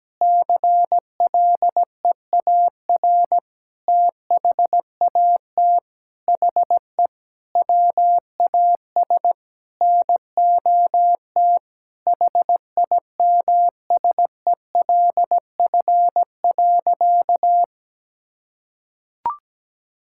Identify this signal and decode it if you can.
Morse code, heard as a keyed tone, about 17 wpm, 700 Hz. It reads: CLEAR THAT HE WAS NOT HIMSELF.
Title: Sentences from Top 1600 Words 17wpm